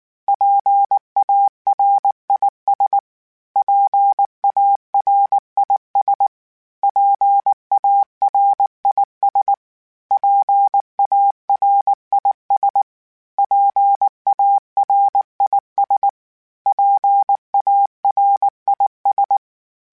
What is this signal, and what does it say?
Morse code, heard as a keyed tone, about 19 wpm, 800 Hz. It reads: PARIS PARIS PARIS PARIS PARIS PARIS
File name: PARIS Calibration / PARIS mp3s / PARIS@18